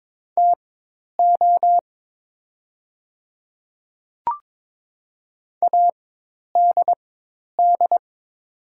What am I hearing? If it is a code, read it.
Morse code, heard as a keyed tone, about 22 wpm, spaced as for 10 wpm, 700 Hz. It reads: TO ADD
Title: 4x Character Spacing - Top 200 Words - 22wpm